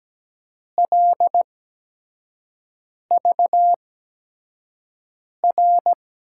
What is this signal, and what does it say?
Morse code, heard as a keyed tone, about 17 wpm, spaced as for 5 wpm, 700 Hz. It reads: LVR
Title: USA Call Signs - 8x Character Spacing - 17wpm